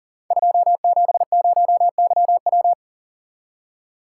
Morse code, heard as a keyed tone, about 38 wpm, 700 Hz. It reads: 270YW
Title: Code Groups - 5 Letter:Numbers - 40wpm